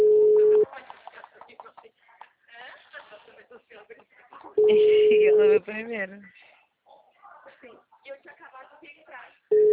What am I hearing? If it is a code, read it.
Ringback tone (Europe, 425 Hz), with speech.
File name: out- -2017- - - .90597